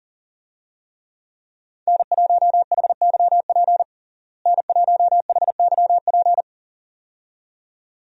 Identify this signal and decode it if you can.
Morse code, heard as a keyed tone, about 40 wpm, 700 Hz. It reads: N1HYP N1HYP
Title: USA Call Signs - 5 Characters - Repeated 2x - 40wpm